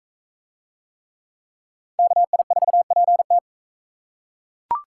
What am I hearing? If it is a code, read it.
Morse code, heard as a keyed tone, about 42 wpm, 700 Hz. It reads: KI4PT